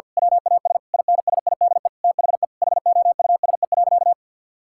Morse code, heard as a keyed tone, about 50 wpm, 700 Hz. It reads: WAS INSIDE THE HOUSE.
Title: Sentences from Top 1000 Words - Review Part 6 - 50wpm